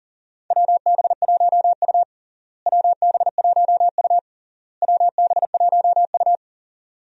Morse code, heard as a keyed tone, about 40 wpm, 700 Hz. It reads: WB1U WB1U WB1U
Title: USA Call Signs - 4 Characters - Repeated 3x - 40wpm